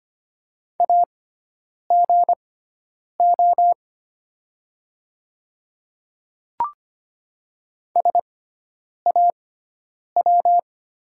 Morse code, heard as a keyed tone, about 25 wpm, spaced as for 9 wpm, 700 Hz. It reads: AGO SAW
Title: 6x Character Spacing - Top 500 Words - 25wpm